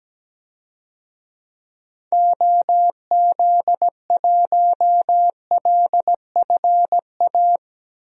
Morse code, heard as a keyed tone, about 17 wpm, 700 Hz. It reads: OZ1LFA
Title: DXCC-100 Call Signs - 6 Characters - Repeated 3x - 17wpm